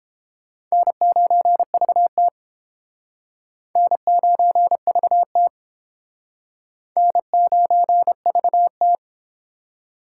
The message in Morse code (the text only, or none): N9VT N9VT N9VT